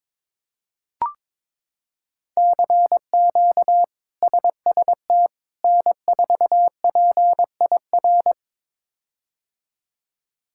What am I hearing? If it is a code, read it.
Morse code, heard as a keyed tone, about 22 wpm, 700 Hz. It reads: CQ SST N4PIR
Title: K1USN SST QSO Practice - 22wpm